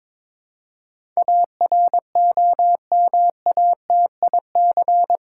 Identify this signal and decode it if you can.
Morse code, heard as a keyed tone, about 22 wpm, 700 Hz. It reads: AROMATIC
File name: Difficult Words with Hints - 22wpm